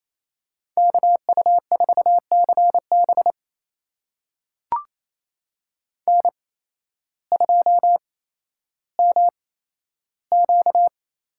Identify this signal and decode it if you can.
Morse code, heard as a keyed tone, about 28 wpm, spaced as for 13 wpm, 700 Hz. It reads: KU4CB N2MQ
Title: USA Call Signs - 5 Characters - 8x Character Spacing - 28wpm